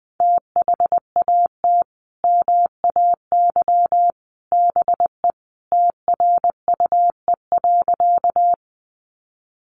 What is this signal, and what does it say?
Morse code, heard as a keyed tone, about 20 wpm, 700 Hz. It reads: THAT MAY BE TRUE.
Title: Sentences from Top 400 Words 20wpm